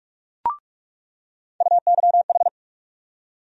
Morse code, heard as a keyed tone, about 46 wpm, 700 Hz. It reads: UYH
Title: USA Call Sign Postfixes - 3 Letters 45wpm